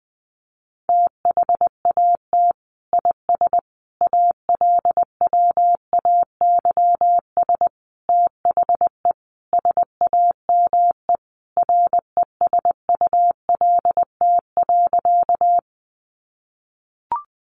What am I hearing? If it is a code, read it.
Morse code, heard as a keyed tone, about 19 wpm, 700 Hz. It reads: THAT IS ALWAYS THE SAME RESULT.
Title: Sentences from Top 700 Words 20wpm